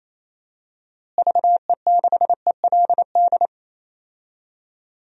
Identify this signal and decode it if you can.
Morse code, heard as a keyed tone, about 28 wpm, 700 Hz. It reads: VE6ELD
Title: DXCC-100 Call Signs - 6 Characters - Repeated 3x - 28wpm